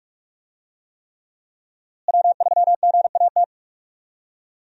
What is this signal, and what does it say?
Morse code, heard as a keyed tone, about 46 wpm, 700 Hz. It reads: W3GAT
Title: USA Call Signs - 5 Characters - Repeated 3x - 45wpm